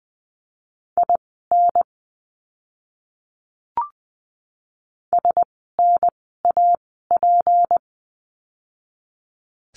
Morse code, heard as a keyed tone, about 20 wpm, spaced as for 14 wpm, 700 Hz. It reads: IN SNAP
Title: 153 Lesson 25f - Letter p - Words 20wpm